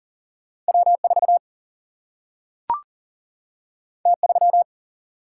Morse code, heard as a keyed tone, about 40 wpm, 700 Hz. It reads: W4 T3
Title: Single-Letter Single-Number 40wpm